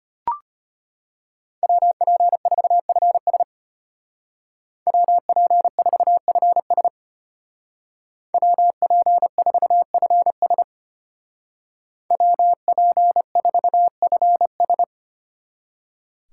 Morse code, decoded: WP4FS WP4FS WP4FS WP4FS

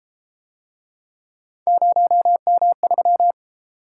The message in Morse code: 0M3

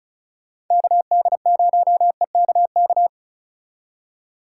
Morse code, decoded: KD0EKK

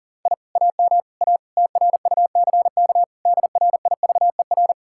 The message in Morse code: I AM A TRUCK DRIVER

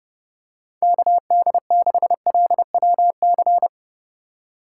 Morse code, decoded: KD6LWC

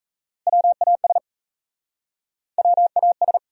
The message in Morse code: WAS WAS

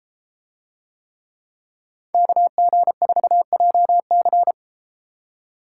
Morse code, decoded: KG4JC

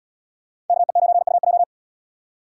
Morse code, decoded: KE9UJ

